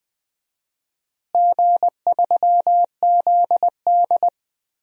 Morse code, heard as a keyed tone, about 20 wpm, 700 Hz. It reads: G3ZD
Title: DXCC-100 Call Signs - 4 Characters - Repeated 3x - 20wpm